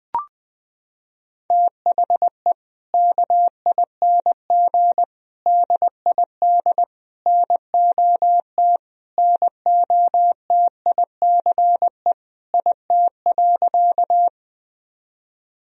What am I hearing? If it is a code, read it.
Morse code, heard as a keyed tone, about 20 wpm, 700 Hz. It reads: THE KING DID NOT NOTICE IT.
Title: Sentences from Top 1000 Words - Review Part 2 - 20wpm